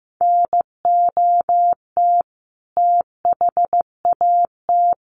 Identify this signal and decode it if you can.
Morse code, heard as a keyed tone, about 15 wpm, 700 Hz. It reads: NOT THAT
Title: A Sound Of Thunder 15wpm